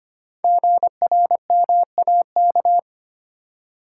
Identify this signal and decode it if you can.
Morse code, heard as a keyed tone, about 25 wpm, 700 Hz. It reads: GRMAK
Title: Code Groups - 5 Letters - 25wpm